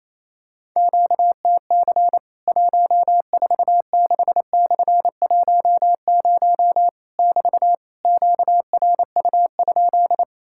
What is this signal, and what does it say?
Morse code, heard as a keyed tone, about 28 wpm, 700 Hz. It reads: QTC 146/10 <BT> QRU?
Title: Worked All Europe DX Contest - QTC Exchange - 28wpm